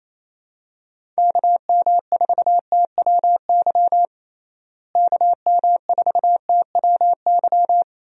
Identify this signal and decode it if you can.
Morse code, heard as a keyed tone, about 28 wpm, 700 Hz. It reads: KM4TWY KM4TWY
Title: USA Call Signs - 6 Characters - Repeated 2x - 28wpm